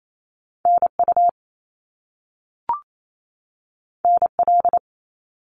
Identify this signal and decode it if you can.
Morse code, heard as a keyed tone, about 28 wpm, 700 Hz. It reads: NU NL
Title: Canadian Provinces and Territories 28wpm